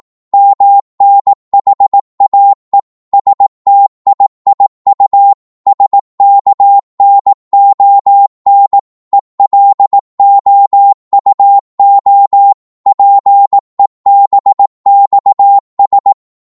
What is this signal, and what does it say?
Morse code, heard as a keyed tone, about 18 wpm, 800 Hz. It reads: MNHAE STIIU SKNON ELOUO PEBXS
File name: Morse / ltr-18wpm